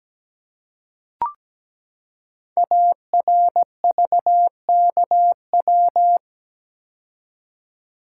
Morse code, heard as a keyed tone, about 17 wpm, 700 Hz. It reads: ARVKW
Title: Code Groups - 5 Letters - 17wpm